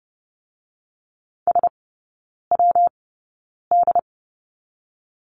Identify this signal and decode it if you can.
Morse code, heard as a keyed tone, about 30 wpm, 700 Hz. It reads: S W D
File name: Lesson 19e - Letter w - Three Characters 30wpm